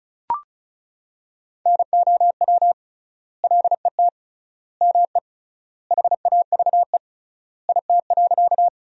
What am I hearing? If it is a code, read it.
Morse code, heard as a keyed tone, about 36 wpm, 700 Hz. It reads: NOW LET ME HAVE IT.
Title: 3x Word-Spacing - Sentences from Top 300 Words - 35wpm